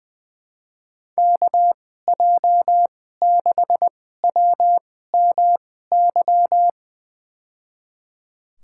Morse code, heard as a keyed tone, about 20 wpm, spaced as for 15 wpm, 700 Hz. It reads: KJ6WMY
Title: USA Call Signs - 2x Character Spacing - 20wpm